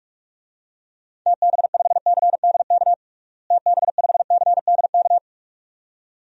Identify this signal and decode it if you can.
Morse code, heard as a keyed tone, about 46 wpm, 700 Hz. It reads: TB5CDK TB5CDK
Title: DXCC-100 Call Signs - 6 Characters - Repeated 2x - 45wpm